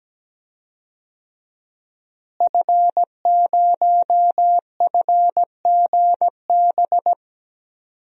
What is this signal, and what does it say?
Morse code, heard as a keyed tone, about 17 wpm, 700 Hz. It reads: F0FGB
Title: DXCC-100 Call Signs - 5 Characters - Repeated 3x - 17wpm